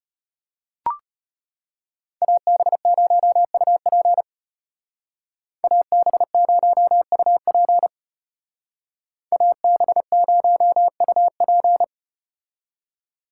Morse code, heard as a keyed tone, about 34 wpm, 700 Hz. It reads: AB0UP AB0UP AB0UP